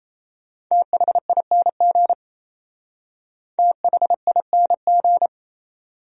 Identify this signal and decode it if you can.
Morse code, heard as a keyed tone, about 35 wpm, 700 Hz. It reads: THING THING
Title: Speed-Racing Top 100 Words 28wpm